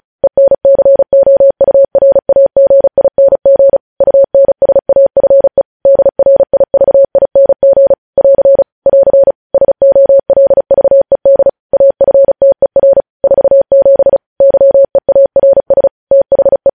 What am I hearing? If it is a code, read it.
Morse code, heard as a keyed tone, about 36 wpm, 550 Hz. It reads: ENCOURAGING UNSAFE DRIVING <AR> <AR> SOLVED AFTER 47 YEARS THE
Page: news headlines in morse code at 35 wpm